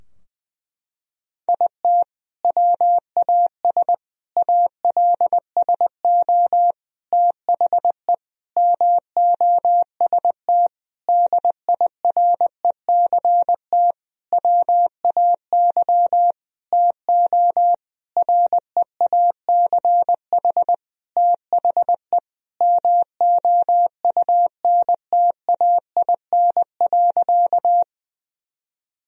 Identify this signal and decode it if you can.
Morse code, heard as a keyed tone, about 20 wpm, 700 Hz. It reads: IT WAS ALSO THE MOST DIRECT WAY TO REACH THE MOUNTAIN.
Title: Sentences from Top 1000 Words - Review Part 2 - 20wpm